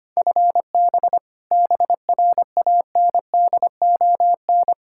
Morse code, heard as a keyed tone, about 25 wpm, 700 Hz. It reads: FB BRANDON
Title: Practice QSO - Exchange 3 - More Info - 25wpm